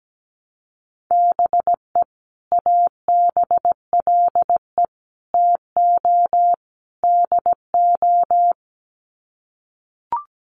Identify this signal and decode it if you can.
Morse code, heard as a keyed tone, about 17 wpm, 700 Hz. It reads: BE ABLE TO DO
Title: Sets of 4 Words 17wpm